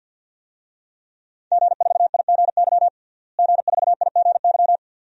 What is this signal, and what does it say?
Morse code, heard as a keyed tone, about 48 wpm, 700 Hz. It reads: G4IGY G4IGY
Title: DXCC-100 Call Signs - 5 Characters - Repeated 2x - 50wpm